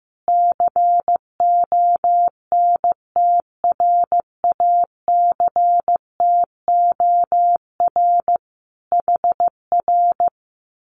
Morse code, heard as a keyed tone, about 15 wpm, 700 Hz. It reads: CONTRACTOR HR
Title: Common QSO Phrases Encore 15wpm